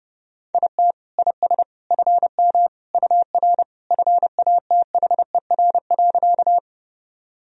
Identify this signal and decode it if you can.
Morse code, heard as a keyed tone, about 30 wpm, 700 Hz. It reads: IT IS FM UR FATHER.
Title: Sentences from Top 300 Words and Abbreviations 30wpm